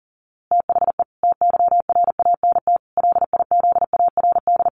Morse code, heard as a keyed tone, about 40 wpm, 700 Hz. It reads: THE TYRANT LIZARD
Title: A Sound Of Thunder 40wpm